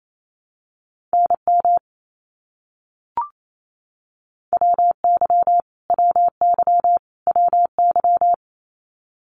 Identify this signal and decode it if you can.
Morse code, heard as a keyed tone, about 28 wpm, 700 Hz. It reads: NM WY WY WY
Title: US States Repeated 3x 28wpm